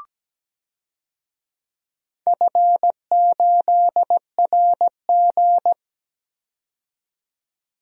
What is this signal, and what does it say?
Morse code, heard as a keyed tone, about 17 wpm, 700 Hz. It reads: F8RG
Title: DXCC-100 Call Signs - 4 Characters - 17wpm